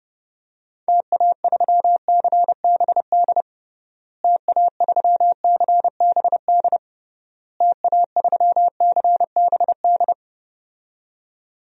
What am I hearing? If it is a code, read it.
Morse code, heard as a keyed tone, about 32 wpm, 700 Hz. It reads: TA3CBD TA3CBD TA3CBD
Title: DXCC-100 Call Signs - 6 Characters - Repeated 3x - 30wpm